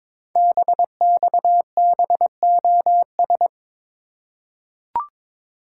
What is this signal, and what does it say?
Morse code, heard as a keyed tone, about 23 wpm, 700 Hz. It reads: BXBOS
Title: Code Groups - 5 Letters - 22wpm